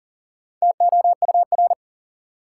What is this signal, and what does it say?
Morse code, heard as a keyed tone, about 40 wpm, 700 Hz. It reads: TOUR
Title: Four Letter Words - 40wpm